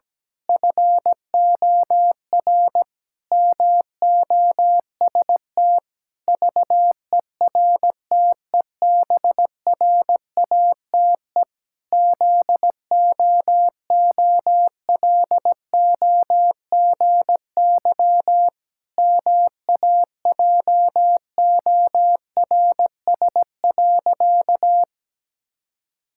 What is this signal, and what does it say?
Morse code, heard as a keyed tone, about 17 wpm, 700 Hz. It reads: FOR MOST VERTEBRATE ZOOLOGY MAJORS.
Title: Pangram Challenge v2 - 17wpm